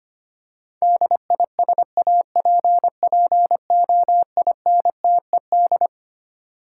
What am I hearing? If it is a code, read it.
Morse code, heard as a keyed tone, about 25 wpm, 700 Hz. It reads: DISAPPOINTED
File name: Twelve Letter Words - 25wpm